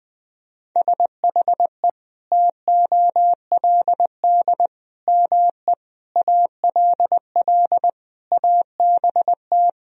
Morse code, heard as a keyed tone, about 21 wpm, 700 Hz. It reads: SHE TOLD ME ALL ABT
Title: Sentences from Top 400 Words and Abbreviations 20wpm